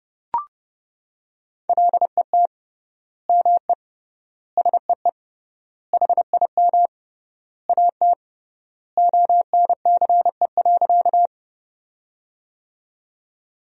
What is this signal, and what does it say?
Morse code, heard as a keyed tone, about 30 wpm, 700 Hz. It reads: LET ME SEE HIM AT ONCE.
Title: 3x Word-Spacing - Sentences from Top 300 Words - 30wpm